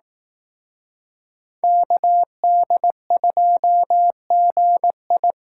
Morse code, heard as a keyed tone, about 18 wpm, 700 Hz. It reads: KD2GI
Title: Speed Racing - USA 2x2 Call Signs 15wpm